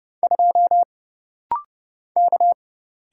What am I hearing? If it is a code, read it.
Morse code, heard as a keyed tone, about 30 wpm, 700 Hz. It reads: 2 K
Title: Single Letter-Number - Rapid-Fire 30wpm